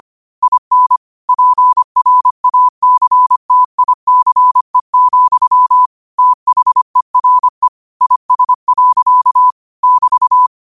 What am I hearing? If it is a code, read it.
Morse code, heard as a keyed tone, about 25 wpm, 1000 Hz. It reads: IN PRACTICE, THERE IS. <BT>
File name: qotdfiles / 25wpm